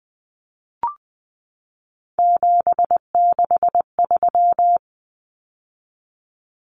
Morse code, heard as a keyed tone, about 19 wpm, 700 Hz. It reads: 763
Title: Numbers - 3-Digit 20wpm